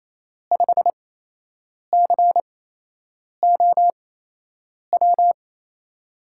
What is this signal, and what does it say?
Morse code, heard as a keyed tone, about 28 wpm, spaced as for 8 wpm, 700 Hz. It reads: 5COW
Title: USA Call Signs - 5 Characters - 8x Character Spacing - 28wpm